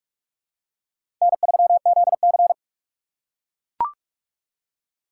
Morse code, heard as a keyed tone, about 44 wpm, 700 Hz. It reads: N3ZC
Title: USA Call Signs - 4 Characters - 2x Character Spacing - 45wpm